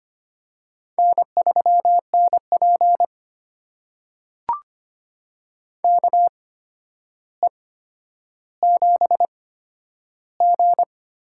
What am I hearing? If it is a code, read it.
Morse code, heard as a keyed tone, about 25 wpm, spaced as for 11 wpm, 700 Hz. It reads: N3NP KE7G
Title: USA Call Signs - 4 Characters - 8x Character Spacing - 25wpm